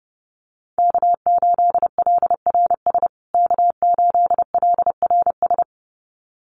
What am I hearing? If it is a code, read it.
Morse code, heard as a keyed tone, about 28 wpm, 700 Hz. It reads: K8LRS K8LRS
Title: USA Call Signs Repeated 2x 30wpm